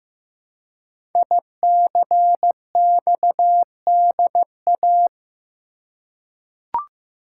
Morse code, heard as a keyed tone, about 15 wpm, 700 Hz. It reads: ICXDA
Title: Code Groups - 5 Letter:Numbers - 15wpm